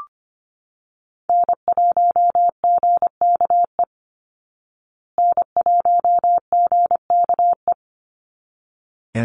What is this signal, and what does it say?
Morse code, heard as a keyed tone, about 25 wpm, 700 Hz. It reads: N1GKE N1GKE
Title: Call Signs Repeated 3x - 25wpm